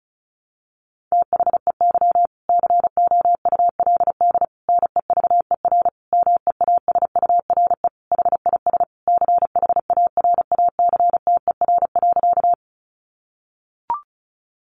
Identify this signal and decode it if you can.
Morse code, heard as a keyed tone, about 34 wpm, 700 Hz. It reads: THEY COULD NEVER MEASURE HIS CHARACTER.
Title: Sentences from Top 1200 Words 35wpm